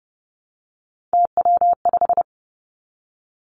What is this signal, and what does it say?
Morse code, heard as a keyed tone, about 30 wpm, 700 Hz. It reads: TW5
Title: Three Challenging Letter-Number Combinations 30wpm